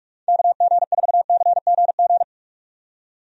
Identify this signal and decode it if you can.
Morse code, heard as a keyed tone, about 46 wpm, 700 Hz. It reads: KG4CGG
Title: USA Call Signs - 6 Characters - 45wpm